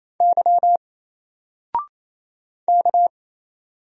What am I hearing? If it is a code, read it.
Morse code, heard as a keyed tone, about 28 wpm, 700 Hz. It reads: Y K
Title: Single Letter-Number 28wpm